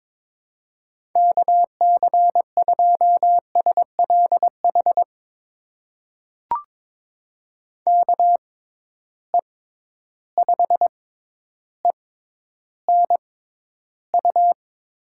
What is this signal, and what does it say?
Morse code, heard as a keyed tone, about 23 wpm, spaced as for 11 wpm, 700 Hz. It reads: KC2SLH KE5ENU
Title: USA Call Signs - 6 Characters - 6x Character Spacing - 22wpm